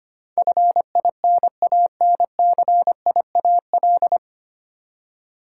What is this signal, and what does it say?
Morse code, heard as a keyed tone, about 25 wpm, 700 Hz. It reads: FINANCIAL
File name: Nine Letter Words - 25wpm